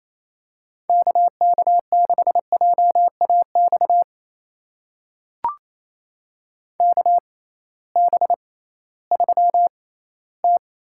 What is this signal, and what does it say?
Morse code, heard as a keyed tone, about 27 wpm, spaced as for 16 wpm, 700 Hz. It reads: KK6JAX KB3T
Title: USA Call Signs - 6 Characters - 6x Character Spacing - 28wpm